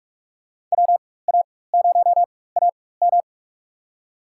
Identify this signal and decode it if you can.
Morse code, heard as a keyed tone, about 47 wpm, spaced as for 21 wpm, 700 Hz. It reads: WA0AM
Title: USA Call Signs - 4x Character Spacing - 45wpm